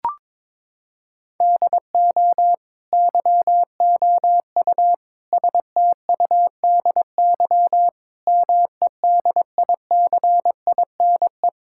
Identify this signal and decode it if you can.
Morse code, heard as a keyed tone, about 22 wpm, 700 Hz. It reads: DO YOU STUDY MEDICINE
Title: Sentences from Top 1000 Words - Review Part 6 - 22wpm